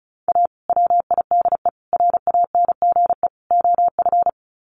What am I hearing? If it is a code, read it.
Morse code, heard as a keyed tone, about 34 wpm, 700 Hz. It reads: A WIDE RANGE OF
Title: Sets of 4 Words 35wpm